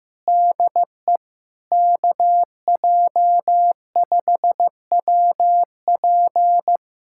Morse code, heard as a keyed tone, about 15 wpm, 690 Hz. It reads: DE KJ5WP
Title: SOTA QSO Practice 15wpm